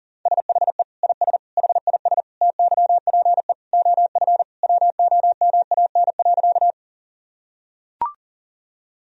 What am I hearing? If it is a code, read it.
Morse code, heard as a keyed tone, about 40 wpm, 700 Hz. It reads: SHE IS HIS TYPE OF WOMAN.
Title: Sentences from Top 1000 Words - Review Part 3 - 40wpm